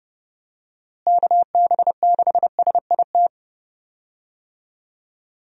Morse code, heard as a keyed tone, about 32 wpm, 700 Hz. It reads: KB6SIT
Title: USA Call Signs - 6 Characters - Repeated 3x - 30wpm